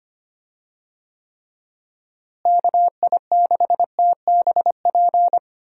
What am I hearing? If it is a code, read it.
Morse code, heard as a keyed tone, about 25 wpm, 700 Hz. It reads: KI6TBP
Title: USA Call Signs - 6 Characters - Repeated 2x - 25wpm